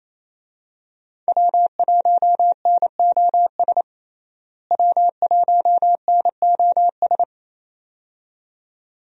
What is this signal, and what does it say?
Morse code, heard as a keyed tone, about 27 wpm, 700 Hz. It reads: W1NOS W1NOS
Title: USA Call Signs - 5 Characters - Repeated 2x - 28wpm